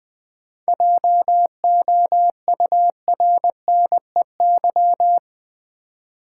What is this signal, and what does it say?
Morse code, heard as a keyed tone, about 20 wpm, 690 Hz. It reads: JOURNEY
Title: 7 Letter Words - First 3 Letters Given - 20wpm